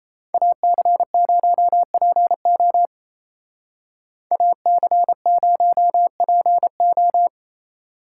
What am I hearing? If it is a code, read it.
Morse code, heard as a keyed tone, about 35 wpm, 700 Hz. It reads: AC0PO AC0PO